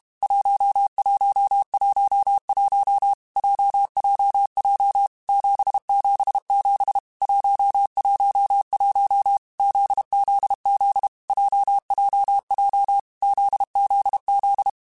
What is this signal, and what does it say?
Morse code, heard as a keyed tone, about 32 wpm, 800 Hz. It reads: 1111 JJJ 777 111 ZZZ JJJ ZZZ
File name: Basic Letter Groups By7Groups / JZ17 / JZ17@30